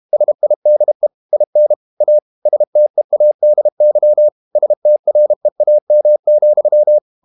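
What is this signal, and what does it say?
Morse code, heard as a keyed tone, about 32 wpm, 600 Hz. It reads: SIDE IN A STEADY STREAM,
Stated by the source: Morse code practice at 32 wpm from CW med Gnister, reading Ken Follett's novel Never in English.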